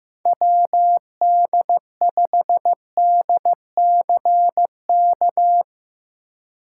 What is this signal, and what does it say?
Morse code, heard as a keyed tone, about 15 wpm, 700 Hz. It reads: WD5DCK
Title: USA Call Signs - 3x Character Spacing - 15wpm